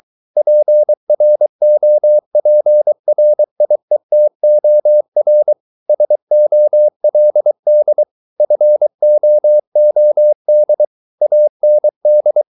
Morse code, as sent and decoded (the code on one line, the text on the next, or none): .--. .-. --- .--. .-. .. - --- .-. / ... --- .-.. -.. / ..-. --- --- -.. / .- -. -..
PROPRIETOR SOLD FOOD AND